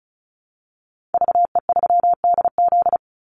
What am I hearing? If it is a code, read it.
Morse code, heard as a keyed tone, about 34 wpm, 700 Hz. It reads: VE3DZ
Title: Call Signs - 35wpm